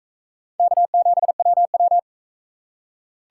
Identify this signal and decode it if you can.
Morse code, heard as a keyed tone, about 39 wpm, 700 Hz. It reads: K7WW